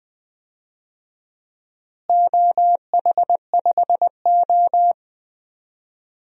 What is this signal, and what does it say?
Morse code, heard as a keyed tone, about 20 wpm, 700 Hz. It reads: OH5O
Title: DXCC-100 Call Signs - 4 Characters - Repeated 2x - 20wpm